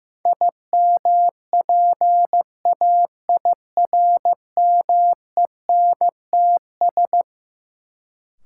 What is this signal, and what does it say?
Morse code, heard as a keyed tone, about 15 wpm, 700 Hz. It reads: IMPAIRMENTS
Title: Sets of 5 Words Encore - New Word Drill - 15wpm